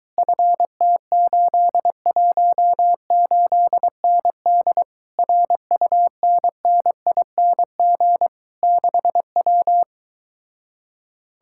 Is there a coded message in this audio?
Yes: Morse code from 0.2 to 9.8 s, heard as a keyed tone, about 22 wpm, 700 Hz. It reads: FT818ND RUNNING 6W